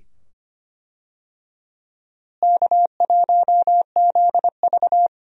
Morse code, heard as a keyed tone, about 25 wpm, 700 Hz. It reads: K1ZV
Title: USA Call Signs - 4 Characters - Repeated 2x - 25wpm